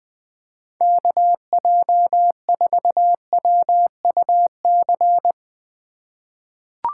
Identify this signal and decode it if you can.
Morse code, heard as a keyed tone, about 20 wpm, 700 Hz. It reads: KJ4WUC
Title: USA Call Signs - 6 Characters - 4x Character Spacing - 20wpm